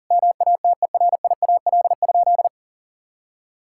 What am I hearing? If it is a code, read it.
Morse code, heard as a keyed tone, about 40 wpm, 700 Hz. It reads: MATERIAL?
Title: Sentences from Top 1000 Words - Review Part 5 - 40wpm